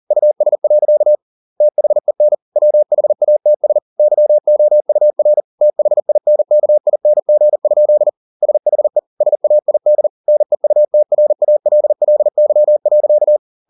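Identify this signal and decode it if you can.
Morse code, heard as a keyed tone, about 40 wpm, 600 Hz. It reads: US. THEN WHATS YOUR THINKING? SHE SAID NEUTRALLY.
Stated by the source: Morse code at 40 wpm, practice text, CW med Gnister.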